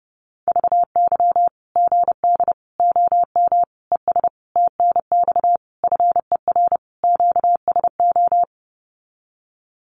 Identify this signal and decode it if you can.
Morse code, heard as a keyed tone, about 30 wpm, 700 Hz. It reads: VY GD OM ES TNX FER QSO
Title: Common QSO Phrases 30wpm